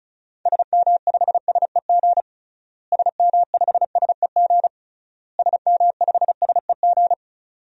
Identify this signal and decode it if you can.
Morse code, heard as a keyed tone, about 34 wpm, 700 Hz. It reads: SM5SEG SM5SEG SM5SEG
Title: DXCC-100 Call Signs - 6 Characters - Repeated 3x - 35wpm